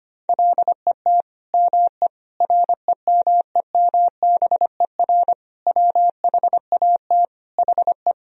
Morse code, heard as a keyed tone, about 25 wpm, 700 Hz. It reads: LET ME REMEMBER WHAT HE
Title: Sentences from Top 400 Words and Abbreviations 25wpm